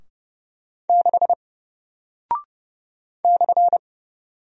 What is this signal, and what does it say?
Morse code, heard as a keyed tone, about 30 wpm, 700 Hz. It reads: 6 /
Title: Single Character (Letter:Number:Punctuation:Prosign) Alt 30wpm